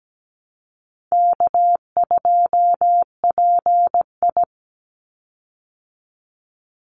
Morse code, heard as a keyed tone, about 17 wpm, 700 Hz. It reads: K2PI
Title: Easy Call Signs Repeated 3x - 17wpm